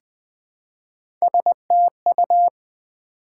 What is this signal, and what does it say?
Morse code, heard as a keyed tone, about 20 wpm, 700 Hz. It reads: STU
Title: Build A Word - 1 Letter At A Time - 4-Letter Words - 20wpm